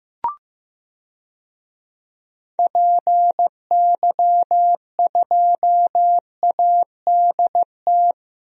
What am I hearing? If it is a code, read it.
Morse code, heard as a keyed tone, about 15 wpm, 700 Hz. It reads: PY2ADT
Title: DXCC-100 Call Signs - 6 Characters - Repeated 2x - 15wpm